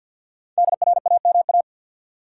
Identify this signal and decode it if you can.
Morse code, heard as a keyed tone, about 52 wpm, 700 Hz. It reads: DRAMA
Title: Sentences from Top 1500 Words - New Word Drill Alt - 50wpm